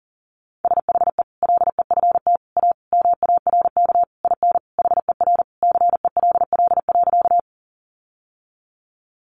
Morse code, heard as a keyed tone, about 40 wpm, 700 Hz. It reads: SHE LEFT A MARK IN HER CELL.
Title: Sentences from Top 1100 Words 40wpm